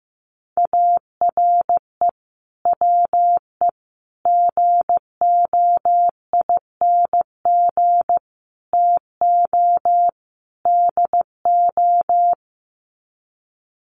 Morse code, heard as a keyed tone, about 15 wpm, 700 Hz. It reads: ARE WE GOING TO DO
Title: Sets of 5 Words 15wpm